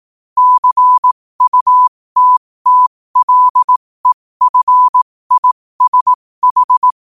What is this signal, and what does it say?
Morse code, heard as a keyed tone, about 18 wpm, spaced as for 15 wpm, 1000 Hz. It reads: CUTTLEFISH
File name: qotdfiles / 15wpm